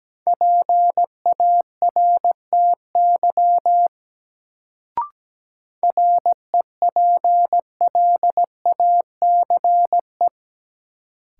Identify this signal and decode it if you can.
Morse code, heard as a keyed tone, about 17 wpm, 700 Hz. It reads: PARTY REPLACE
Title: Sentences from Top 1400 Words - New Word Drill - 17wpm